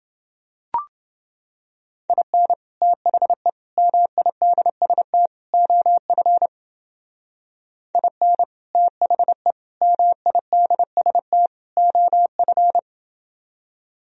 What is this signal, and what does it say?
Morse code, heard as a keyed tone, about 28 wpm, 700 Hz. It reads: IN THE MIDST OF IN THE MIDST OF